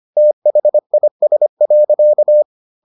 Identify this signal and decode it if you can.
Morse code, heard as a keyed tone, about 25 wpm, 600 Hz. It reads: THIS.